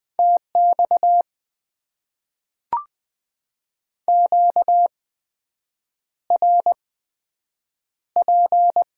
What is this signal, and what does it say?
Morse code, heard as a keyed tone, about 20 wpm, spaced as for 7 wpm, 700 Hz. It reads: TX QRP
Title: Top 90 QSO Elements - 8x Character Spacing - 20wpm